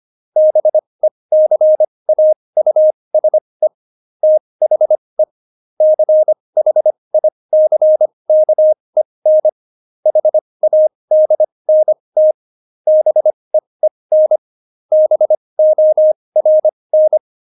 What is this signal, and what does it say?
Morse code, heard as a keyed tone, about 25 wpm, spaced as for 20 wpm, 620 Hz. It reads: BECAUSE THE CHICKEN HADNT BEEN BORN
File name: Jokes-25-char-20-eff